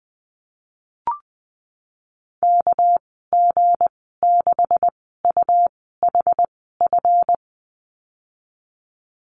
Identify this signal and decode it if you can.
Morse code, heard as a keyed tone, about 20 wpm, spaced as for 15 wpm, 700 Hz. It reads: KG6UHF
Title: 207 Lesson 33g - Letter j - Call Signs 20wpm